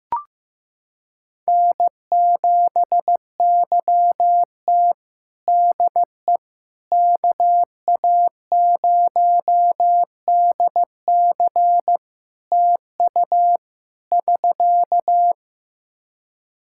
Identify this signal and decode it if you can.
Morse code, heard as a keyed tone, about 15 wpm, 700 Hz. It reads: N7YT DE KA0DC TU <SK>
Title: Practice QSO - Exchange 4 - Ending - 15wpm